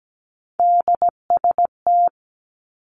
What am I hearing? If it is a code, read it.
Morse code, heard as a keyed tone, about 17 wpm, 700 Hz. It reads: DST
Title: Three Challenging Letter-Number Combinations 17wpm